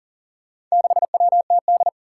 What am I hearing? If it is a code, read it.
Morse code, heard as a keyed tone, about 40 wpm, 700 Hz. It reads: 6WTD
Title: Four Challenging Letter-Number Combinations 40wpm